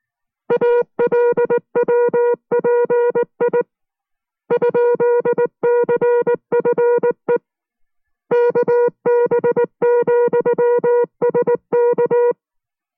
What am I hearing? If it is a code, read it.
Morse code, heard as a keyed tone, about 19 wpm, 480 Hz. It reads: ALWPI ?CFE KB,SK